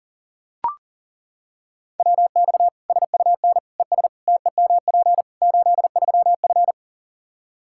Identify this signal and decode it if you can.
Morse code, heard as a keyed tone, about 40 wpm, 700 Hz. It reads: WX SUN ES TEMP 83F
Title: K1USN SST QSO Practice 40wpm